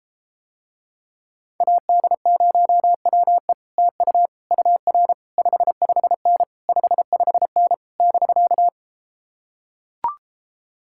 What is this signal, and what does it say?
Morse code, heard as a keyed tone, about 33 wpm, 700 Hz. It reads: AD0WE TU UR 55N 55N <BK>